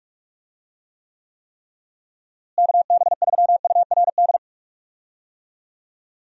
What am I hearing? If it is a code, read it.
Morse code, heard as a keyed tone, about 41 wpm, 700 Hz. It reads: KB3URD
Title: USA Call Signs - 6 Characters - Repeated 3x - 45wpm